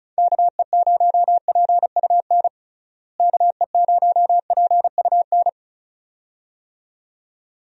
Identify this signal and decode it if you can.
Morse code, heard as a keyed tone, about 36 wpm, 700 Hz. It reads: KE0PUN KE0PUN
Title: USA Call Signs - 6 Characters - Repeated 2x - 35wpm